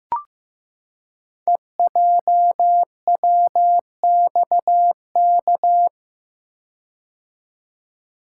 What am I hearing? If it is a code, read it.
Morse code, heard as a keyed tone, about 15 wpm, 700 Hz. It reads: EJWXK
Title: Code Groups - 5 Letters - 15wpm